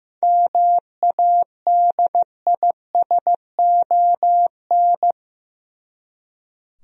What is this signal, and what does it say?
Morse code, heard as a keyed tone, about 15 wpm, 700 Hz. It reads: MADISON
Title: USA Common Names - 15wpm